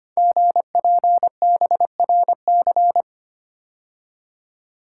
Morse code, heard as a keyed tone, about 25 wpm, 700 Hz. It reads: GPBRC
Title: Code Groups - 5 Letters - 25wpm